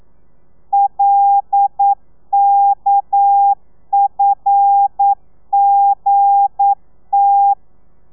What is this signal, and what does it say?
Morse code, heard as a keyed tone, about 9 wpm, 800 Hz. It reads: LKFGT
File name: telegrafia / i2tfj / Lez11 40